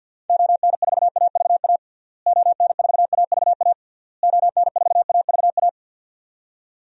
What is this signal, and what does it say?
Morse code, heard as a keyed tone, about 48 wpm, 700 Hz. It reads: ON4AVA ON4AVA ON4AVA
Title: DXCC-100 Call Signs - 6 Characters - Repeated 3x - 50wpm